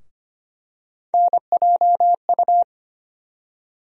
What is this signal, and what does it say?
Morse code, heard as a keyed tone, about 25 wpm, 700 Hz. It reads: NJU
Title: USA Call Sign Postfixes - 3 Letters 25wpm